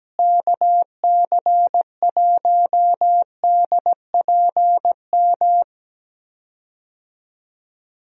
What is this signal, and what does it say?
Morse code, heard as a keyed tone, about 17 wpm, 700 Hz. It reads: KC1DPM
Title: USA Call Signs - 6 Characters - Repeated 3x - 17wpm